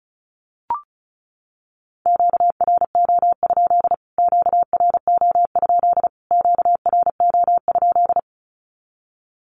Morse code, heard as a keyed tone, about 34 wpm, 700 Hz. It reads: QRO? QRO? QRO?
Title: Q-Codes Repeated 3x 35wpm